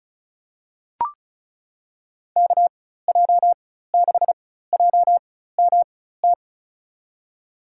Morse code, heard as a keyed tone, about 36 wpm, spaced as for 16 wpm, 700 Hz. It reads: KJ6JMT
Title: USA Call Signs - 6 Characters - 4x Character Spacing - 35wpm